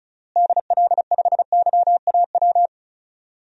Morse code, heard as a keyed tone, about 36 wpm, 700 Hz. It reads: DL5YAW
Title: DXCC-100 Call Signs - 6 Characters - Repeated 3x - 35wpm